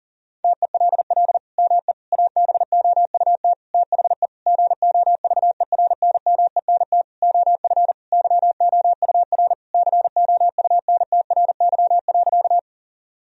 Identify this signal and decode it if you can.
Morse code, heard as a keyed tone, about 40 wpm, 700 Hz. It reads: TELL ME ABOUT THE GOVERNMENT OF YOUR COUNTRY.
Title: Sentences from Top 1000 Words - Review Part 3 - 40wpm